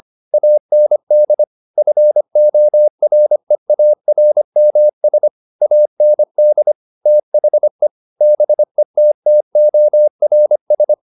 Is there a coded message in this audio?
Yes: Morse code from 0.3 to 10.9 s, heard as a keyed tone, about 25 wpm, 600 Hz. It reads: AND FOREARMS AND THE BETTORS